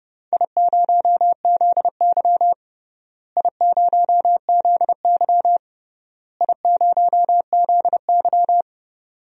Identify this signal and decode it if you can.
Morse code, heard as a keyed tone, about 30 wpm, 700 Hz. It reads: I0ZY I0ZY I0ZY
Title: DXCC-100 Call Signs - 4 Characters - Repeated 3x - 30wpm